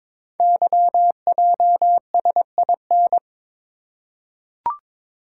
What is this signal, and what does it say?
Morse code, heard as a keyed tone, about 22 wpm, 700 Hz. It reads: YJSIN
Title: Code Groups - 5 Letters - 22wpm